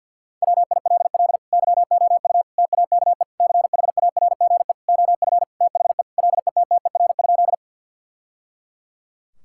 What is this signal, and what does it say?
Morse code, heard as a keyed tone, about 52 wpm, 700 Hz. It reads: WILL YOU TAKE CHARGE OF THE LETTER?
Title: Sentences from Top 1000 Words - Review Part 6 - 50wpm